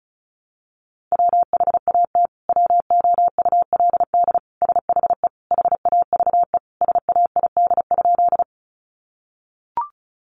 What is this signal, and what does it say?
Morse code, heard as a keyed tone, about 34 wpm, 700 Hz. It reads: WHAT WOULD SHE HAVE SAID?
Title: Sentences from Top 100 Words 35wpm